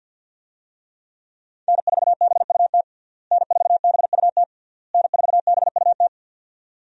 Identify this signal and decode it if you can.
Morse code, heard as a keyed tone, about 49 wpm, 700 Hz. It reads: N4BUT N4BUT N4BUT
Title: USA Call Signs - 5 Characters - Repeated 3x - 50wpm